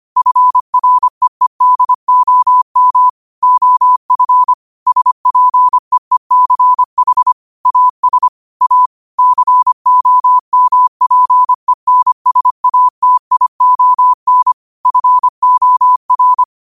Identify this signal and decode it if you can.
Morse code, heard as a keyed tone, about 25 wpm, 1000 Hz. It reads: FREEDOM OF SPEECH AS A COMPENSATION FOR
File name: qotdfiles / 25wpm